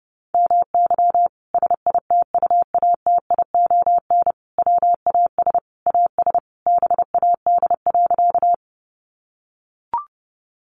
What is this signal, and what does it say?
Morse code, heard as a keyed tone, about 30 wpm, 700 Hz. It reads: MY SITUATION WAS AS BAD.
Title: Sentences from Top 1200 Words 30wpm